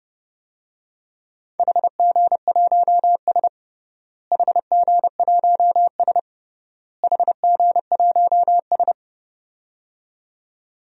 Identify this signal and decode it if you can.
Morse code, heard as a keyed tone, about 30 wpm, 700 Hz. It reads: HG1S HG1S HG1S
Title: DXCC-100 Call Signs - 4 Characters - Repeated 3x - 30wpm